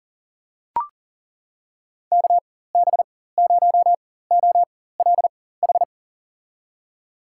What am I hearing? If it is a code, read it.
Morse code, heard as a keyed tone, about 40 wpm, spaced as for 19 wpm, 700 Hz. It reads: KB0OLH
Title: USA Call Signs - 4x Character Spacing - 40wpm